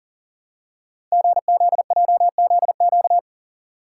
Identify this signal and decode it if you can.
Morse code, heard as a keyed tone, about 40 wpm, 690 Hz. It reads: GZJZQ
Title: Code Groups - 5 Letter:Numbers - 40wpm